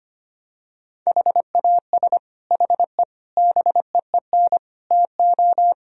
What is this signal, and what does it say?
Morse code, heard as a keyed tone, about 25 wpm, 700 Hz. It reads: HAS HE BEEN TO
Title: Sentences from Top 1000 Words - Review Part 1 - 25wpm